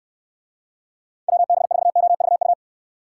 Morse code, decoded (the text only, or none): WB4OVU